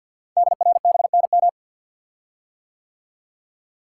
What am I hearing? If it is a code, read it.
Morse code, heard as a keyed tone, about 50 wpm, 700 Hz. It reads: DRBNM
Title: Code Groups - 5 Letters - 50wpm